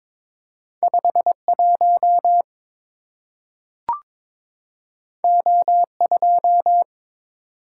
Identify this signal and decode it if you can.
Morse code, heard as a keyed tone, about 22 wpm, 700 Hz. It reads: 51 O2
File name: practice / 22wpm / Two Letter-Number Combinations - 22wpm